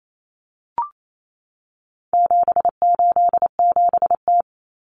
Morse code, heard as a keyed tone, about 28 wpm, 700 Hz. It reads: 787T
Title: Common Cut Numbers - 4-Digit 28wpm